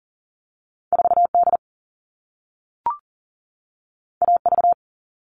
Morse code, heard as a keyed tone, about 42 wpm, 700 Hz. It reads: 4D AV
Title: All Cut Numbers - 2-Digit 40wpm